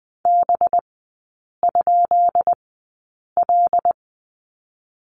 Morse code, heard as a keyed tone, about 20 wpm, 700 Hz. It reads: B ? L
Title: 198 Lesson 32e - Letter k - Three Characters 20wpm